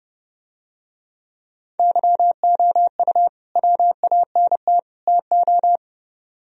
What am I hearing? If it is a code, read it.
Morse code, heard as a keyed tone, about 29 wpm, 700 Hz. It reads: YOU WANT TO